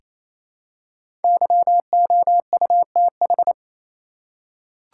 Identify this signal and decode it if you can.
Morse code, heard as a keyed tone, about 28 wpm, 700 Hz. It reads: YOUTH
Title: Five Letter Words - 28wpm